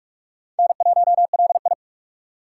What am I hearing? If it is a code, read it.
Morse code, heard as a keyed tone, about 44 wpm, 700 Hz. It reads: N1LI
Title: USA Call Signs - 4 Characters - 3x Character Spacing - 45wpm